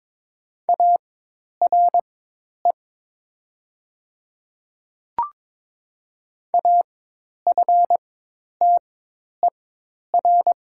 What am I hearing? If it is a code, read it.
Morse code, heard as a keyed tone, about 22 wpm, spaced as for 10 wpm, 700 Hz. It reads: ARE AFTER
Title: 4x Character Spacing - Top 200 Words - 22wpm